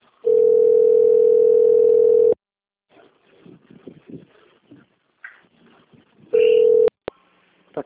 Ringback tone (North America), 440 + 480 Hz.